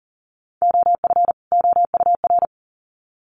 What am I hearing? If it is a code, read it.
Morse code, heard as a keyed tone, about 40 wpm, 700 Hz. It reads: OF OUR
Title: Sets of 2 Words 40wpm